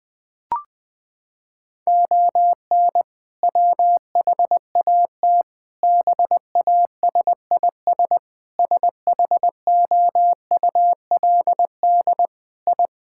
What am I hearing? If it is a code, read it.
Morse code, heard as a keyed tone, about 20 wpm, 700 Hz. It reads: ON WHAT BASIS SHOULD I
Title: Sentences from Top 1000 Words - Review Part 5 - 20wpm